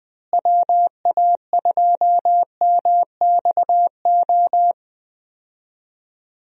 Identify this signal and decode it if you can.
Morse code, heard as a keyed tone, about 21 wpm, 700 Hz. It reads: WA2MXO
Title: USA Call Signs - 6 Characters - Repeated 2x - 20wpm